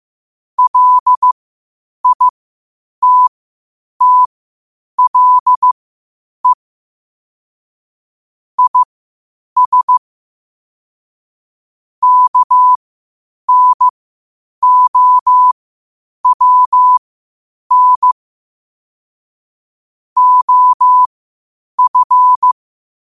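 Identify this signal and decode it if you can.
Morse code, heard as a keyed tone, about 15 wpm, spaced as for 8 wpm, 1000 Hz. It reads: LITTLE IS KNOWN OF
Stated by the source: Morse code at 8 wpm.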